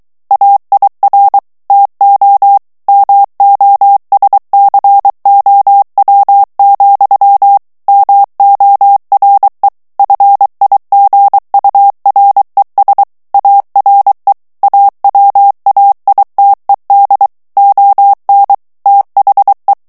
Morse code, heard as a keyed tone, about 23 wpm, 800 Hz. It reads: AIR TO MOSCOW, MORE FIGURES ARE AWAITED ON THE
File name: morse mp3 / ANGLAIS texte clair 2 vitesse 22 mots minute